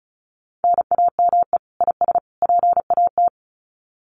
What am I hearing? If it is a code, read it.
Morse code, heard as a keyed tone, about 34 wpm, 700 Hz. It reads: NAME IS PAT
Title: Common QSO Phrases 35wpm